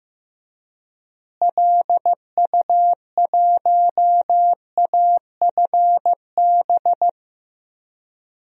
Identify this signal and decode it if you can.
Morse code, heard as a keyed tone, about 15 wpm, 700 Hz. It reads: LU1AFB
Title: DXCC-100 Call Signs - 6 Characters - Repeated 3x - 15wpm